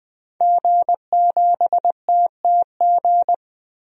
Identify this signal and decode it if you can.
Morse code, heard as a keyed tone, about 20 wpm, 700 Hz. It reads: G7TTG
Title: DXCC-100 Call Signs - 5 Characters - 20wpm